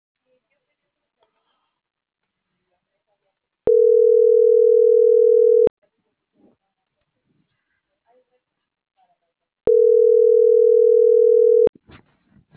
Ringback tone (North America), 440 + 480 Hz.